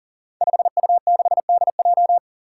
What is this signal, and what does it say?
Morse code, heard as a keyed tone, about 40 wpm, 700 Hz. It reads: 5U6DJ
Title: Five Challenging Letter-Number Combinations 40wpm